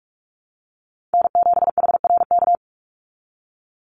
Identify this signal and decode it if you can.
Morse code, heard as a keyed tone, about 44 wpm, 700 Hz. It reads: N7HRK
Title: Call Signs Repeated 2x - 45wpm